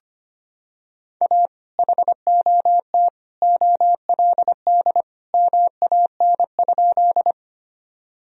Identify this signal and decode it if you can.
Morse code, heard as a keyed tone, about 25 wpm, 700 Hz. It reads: A HOT OLD MAN?
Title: Sentences from Top 1000 Words - Review Part 3 - 25wpm